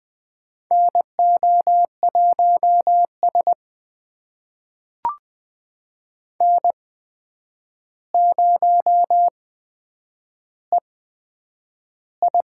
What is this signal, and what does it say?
Morse code, heard as a keyed tone, about 20 wpm, spaced as for 9 wpm, 700 Hz. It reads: NO1S N0EI
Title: USA Call Signs - 4 Characters - 8x Character Spacing - 20wpm